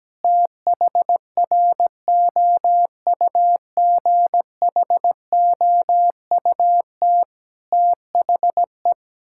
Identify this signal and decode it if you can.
Morse code, heard as a keyed tone, about 17 wpm, 700 Hz. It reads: THROUGHOUT THE
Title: Sentences from Top 1000 Words - Review Part 5 - 17wpm